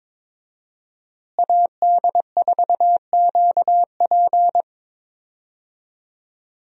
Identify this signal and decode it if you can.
Morse code, heard as a keyed tone, about 22 wpm, 700 Hz. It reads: AD4QP
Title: USA Call Signs - 5 Characters - Repeated 3x - 22wpm